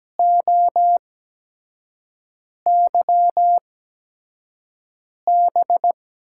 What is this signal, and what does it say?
Morse code, heard as a keyed tone, about 17 wpm, spaced as for 5 wpm, 700 Hz. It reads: OYB
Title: USA Call Signs - 5 Characters - 8x Character Spacing - 17wpm